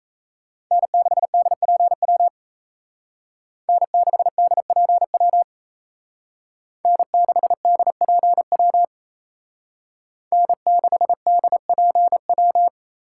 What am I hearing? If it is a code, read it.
Morse code, heard as a keyed tone, about 36 wpm, 700 Hz. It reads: N6DPW N6DPW N6DPW N6DPW